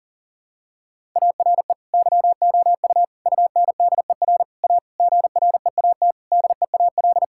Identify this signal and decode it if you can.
Morse code, heard as a keyed tone, about 40 wpm, 690 Hz. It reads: ARE YOU UNDER A GREAT DEAL
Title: Sentences from Top 1000 Words - Review Part 6 - 40wpm